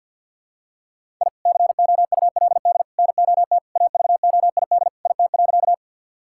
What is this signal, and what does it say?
Morse code, heard as a keyed tone, about 52 wpm, 700 Hz. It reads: I COULD NOT AVOID IT.
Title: Sentences from Top 1000 Words - Review Part 6 - 50wpm